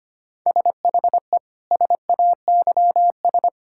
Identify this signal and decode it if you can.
Morse code, heard as a keyed tone, about 25 wpm, 700 Hz. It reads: SHE SAYS
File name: practice / 25wpm / Sentences with 2-Word Phrases - 25wpm